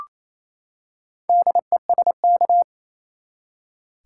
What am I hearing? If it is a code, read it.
Morse code, heard as a keyed tone, about 28 wpm, 700 Hz. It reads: DESK
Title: Four Letter Words - 28wpm